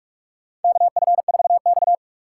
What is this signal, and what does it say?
Morse code, heard as a keyed tone, about 43 wpm, 700 Hz. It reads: KF4X
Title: USA Call Signs - 4 Characters - 4x Character Spacing - 45wpm